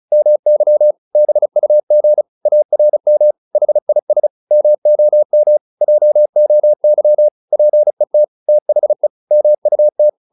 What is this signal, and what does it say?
Morse code, heard as a keyed tone, about 36 wpm, 600 Hz. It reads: MY BUG ARM HIS MOM JOY PET THE MUT